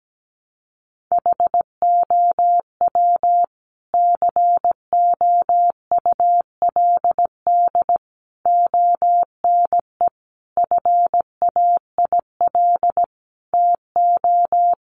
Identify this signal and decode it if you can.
Morse code, heard as a keyed tone, about 17 wpm, 700 Hz. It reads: HOW COULD ONE FAIL TO
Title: Sentences from Top 1100 Words 17wpm